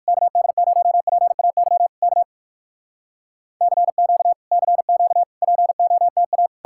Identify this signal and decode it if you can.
Morse code, heard as a keyed tone, about 67 wpm, spaced as for 57 wpm, 700 Hz. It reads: KD0PAY K CQ CQ POTA